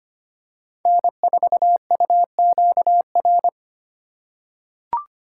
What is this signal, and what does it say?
Morse code, heard as a keyed tone, about 25 wpm, 700 Hz. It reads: N4UQR
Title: USA Call Signs - 4x Character Spacing - 25wpm